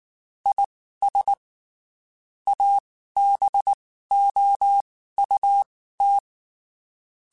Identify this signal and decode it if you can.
Morse code, heard as a keyed tone, about 19 wpm, spaced as for 14 wpm, 800 Hz. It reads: IS ABOUT
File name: (spaced) / QSOs Spaced-mp3s / WD0FBT 1 @18 sp